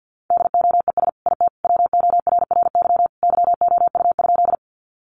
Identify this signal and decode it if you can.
Morse code, heard as a keyed tone, about 49 wpm, 700 Hz. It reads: DOES IT WORRY YOU?
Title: Sentences from Top 1100 Words 50wpm